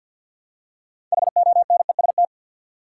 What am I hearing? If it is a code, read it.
Morse code, heard as a keyed tone, about 53 wpm, 700 Hz. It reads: HONEST
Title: Six Letter Words - 50wpm